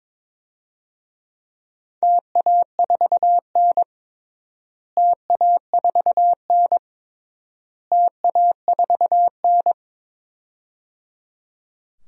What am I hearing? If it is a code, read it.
Morse code, heard as a keyed tone, about 22 wpm, 700 Hz. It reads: TA4N TA4N TA4N
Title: DXCC-100 Call Signs - 4 Characters - Repeated 3x - 22wpm